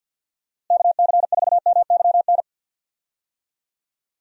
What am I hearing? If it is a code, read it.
Morse code, heard as a keyed tone, about 50 wpm, 700 Hz. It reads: KC4MYN
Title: USA Call Signs - 6 Characters - Repeated 2x - 50wpm